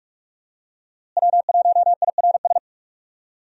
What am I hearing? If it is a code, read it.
Morse code, heard as a keyed tone, about 46 wpm, 700 Hz. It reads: W1IRS
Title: USA Call Signs - 5 Characters - Repeated 2x - 45wpm